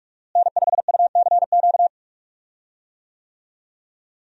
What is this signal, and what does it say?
Morse code, heard as a keyed tone, about 41 wpm, 700 Hz. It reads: N5UCQ
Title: USA Call Signs - 5 Characters - Repeated 3x - 45wpm